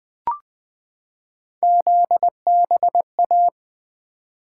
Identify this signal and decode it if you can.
Morse code, heard as a keyed tone, about 20 wpm, 700 Hz. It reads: ZBA
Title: Three Letter-Number Combinations - 20wpm